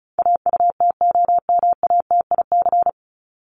Morse code, heard as a keyed tone, about 34 wpm, 700 Hz. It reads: AUTOMATIC
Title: Top 1751-2000 Words 35wpm